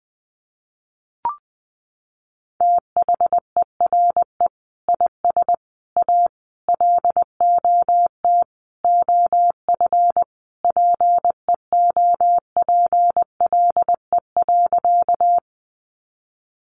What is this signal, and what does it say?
Morse code, heard as a keyed tone, about 20 wpm, 700 Hz. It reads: THERE IS A LOT OF PEOPLE.
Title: Sentences from Top 500 Words 20wpm